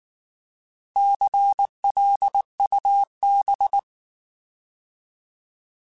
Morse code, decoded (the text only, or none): CLUB